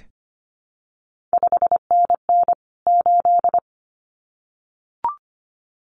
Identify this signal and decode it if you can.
Morse code, heard as a keyed tone, about 25 wpm, 700 Hz. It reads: 5NN 8
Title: CQ WW DX Contest Exchange Encore - 25wpm